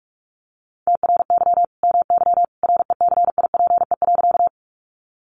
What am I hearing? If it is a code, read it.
Morse code, heard as a keyed tone, about 44 wpm, 700 Hz. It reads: TRY MY RECIPE.
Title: Sentences from Top 1200 Words 45wpm